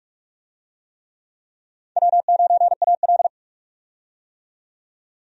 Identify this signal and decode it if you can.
Morse code, heard as a keyed tone, about 46 wpm, 700 Hz. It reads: W9AL
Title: USA Call Signs - 4 Characters - 45wpm